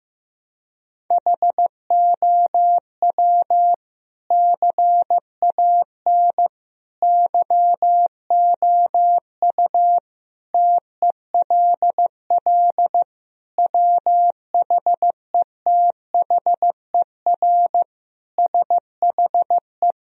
Morse code, heard as a keyed tone, about 15 wpm, 700 Hz. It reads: HOW CAN YOU TELL WHETHER SHE
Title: Sentences from Top 1000 Words - Review Part 3 - 15wpm